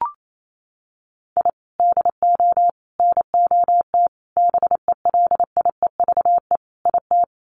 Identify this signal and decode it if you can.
Morse code, heard as a keyed tone, about 28 wpm, 700 Hz. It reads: I DO NOT BELIEVE IT
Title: Sets of 5 Words 28wpm